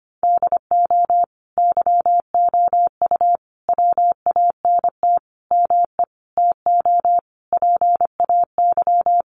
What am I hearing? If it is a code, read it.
Morse code, heard as a keyed tone, about 25 wpm, 700 Hz. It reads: DO YOU WANT ME TO PAY
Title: Sentences from Top 1400 Words 25wpm